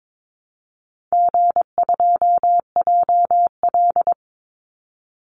Morse code, heard as a keyed tone, about 22 wpm, 700 Hz. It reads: G2JL
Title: Call Signs - 22wpm